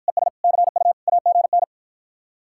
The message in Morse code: ES CU AGN

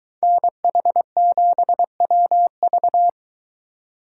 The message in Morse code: NH7WV

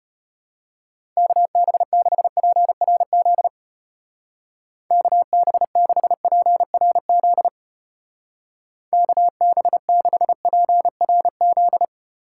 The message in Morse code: KB6PRZ KB6PRZ KB6PRZ

